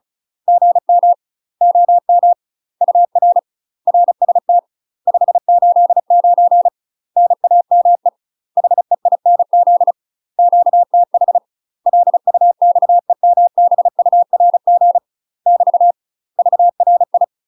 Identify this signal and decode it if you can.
Morse code, heard as a keyed tone, about 34 wpm, 700 Hz. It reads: GM OM UR RST 589 NAME HEINZ QTH LUXEMBURG <BT> VRI
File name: UK MASTER QSOs TRAINING / 35WPM / UK MASTER QSOs 0021